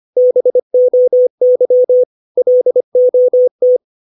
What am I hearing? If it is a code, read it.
Morse code, heard as a keyed tone, about 25 wpm, 500 Hz. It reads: BOY LOT
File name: 632 words / 632 Words 25WPM